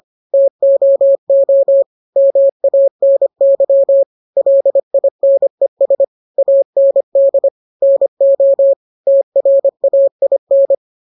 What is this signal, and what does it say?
Morse code, heard as a keyed tone, about 25 wpm, 550 Hz. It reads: TOO MANY LINES AND NO TRAIN